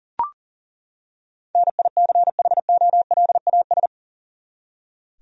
Morse code, heard as a keyed tone, about 37 wpm, 700 Hz. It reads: NICHOLAS
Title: USA Common Names - 40wpm